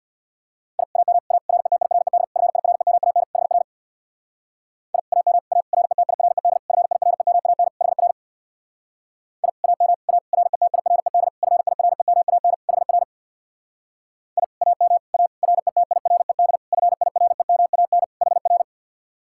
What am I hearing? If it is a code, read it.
Morse code, heard as a keyed tone, about 60 wpm, 700 Hz. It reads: I AM A RETIRED FIREMAN HR I AM A RETIRED FIREMAN HR I AM A RETIRED FIREMAN HR I AM A RETIRED FIREMAN HR
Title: Speed-Racing Common QSO Phrases - 50wpm